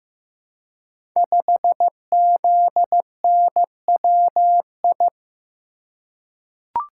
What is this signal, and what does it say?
Morse code, heard as a keyed tone, about 15 wpm, 700 Hz. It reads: 5ZNWI
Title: Code Groups - 5 Letter:Numbers - 15wpm